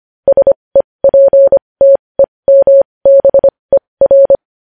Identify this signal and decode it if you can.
Morse code, heard as a keyed tone, about 25 wpm, spaced as for 20 wpm, 550 Hz. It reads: SEPTEMBER
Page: News Headlines in Morse Code at 20 WPM